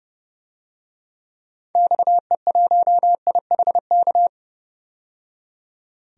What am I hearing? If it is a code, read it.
Morse code, heard as a keyed tone, about 30 wpm, 700 Hz. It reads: XE1IHK
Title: DXCC-100 Call Signs - 6 Characters - 30wpm